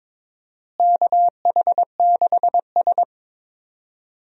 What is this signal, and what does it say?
Morse code, heard as a keyed tone, about 22 wpm, 700 Hz. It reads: KH6S
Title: DXCC-100 Call Signs - 4 Characters - Repeated 3x - 22wpm